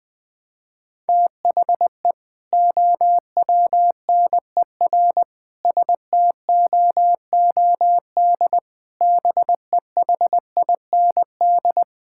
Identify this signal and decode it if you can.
Morse code, heard as a keyed tone, about 21 wpm, 700 Hz. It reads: THE OWNER STOOD BEHIND
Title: Sentences from Top 1600 Words 20wpm